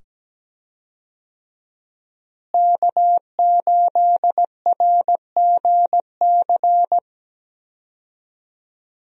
Morse code, heard as a keyed tone, about 17 wpm, 700 Hz. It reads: K8RGC
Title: USA Call Signs - 5 Characters - Repeated 2x - 17wpm